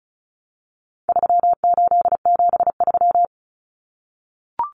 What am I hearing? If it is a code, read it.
Morse code, heard as a keyed tone, about 34 wpm, 700 Hz. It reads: 3873
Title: Numbers - 4-Digit 35wpm